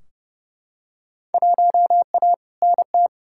Morse code, heard as a keyed tone, about 31 wpm, 700 Hz. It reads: 1A NT
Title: Field Day Exchanges Encore - 30wpm